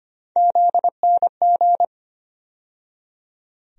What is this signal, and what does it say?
Morse code, heard as a keyed tone, about 25 wpm, 700 Hz. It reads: ZNG
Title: USA Call Sign Postfixes - 3 Letters 25wpm